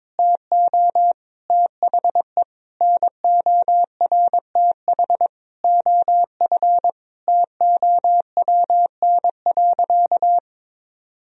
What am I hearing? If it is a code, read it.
Morse code, heard as a keyed tone, about 22 wpm, 700 Hz. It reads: TO THE NORTH OF TOWN.
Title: Sentences from Top 1000 Words - Review Part 4 - 22wpm